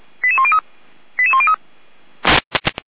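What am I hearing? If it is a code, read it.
Five-tone selective call (ZVEI), heard as quick runs of five tones, tones of about 80 ms. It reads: 88103 88103
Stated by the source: ZVEI five-tone selective call, code 88104, received via SDR by fire brigade Feuerwehr Orlamünde.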